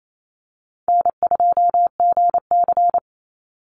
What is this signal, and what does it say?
Morse code, heard as a keyed tone, about 28 wpm, 700 Hz. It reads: N2GC
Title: Easy Call Signs - 28wpm